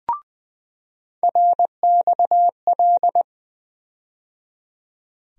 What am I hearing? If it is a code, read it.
Morse code, heard as a keyed tone, about 20 wpm, 700 Hz. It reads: RXL
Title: USA Call Sign Postfixes - 3 Letters 20wpm